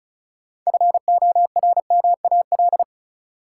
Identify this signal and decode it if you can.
Morse code, heard as a keyed tone, about 36 wpm, 700 Hz. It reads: FORMAL
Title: Sentences from Top 1300 Words - New Word Drill - 35wpm